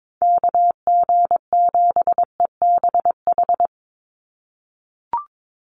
Morse code, heard as a keyed tone, about 22 wpm, 700 Hz. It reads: KG7EBH
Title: USA Call Signs 22wpm